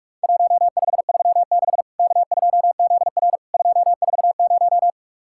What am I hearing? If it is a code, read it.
Morse code, heard as a keyed tone, about 46 wpm, 700 Hz. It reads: 1536 K2ZR 240
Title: Worked All Europe DX Contest - QTC Exchange - 45wpm